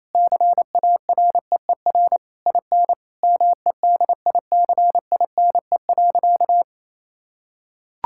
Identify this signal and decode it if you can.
Morse code, heard as a keyed tone, about 28 wpm, 700 Hz. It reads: CAREER IN MEDICINE.
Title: Sentences from Top 1000 Words - Review Part 6 - 28wpm